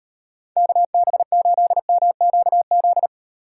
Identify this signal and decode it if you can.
Morse code, heard as a keyed tone, about 38 wpm, 700 Hz. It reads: KB8MQZ